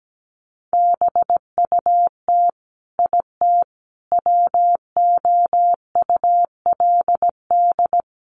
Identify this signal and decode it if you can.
Morse code, heard as a keyed tone, about 17 wpm, 700 Hz. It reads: BUT IT WOULD
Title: Sentences from Top 1500 Words 17wpm